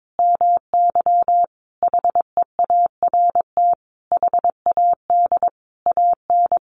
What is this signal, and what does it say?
Morse code, heard as a keyed tone, about 22 wpm, 700 Hz. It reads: MY HEART HAD AN
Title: Sentences from Top 1200 Words 22wpm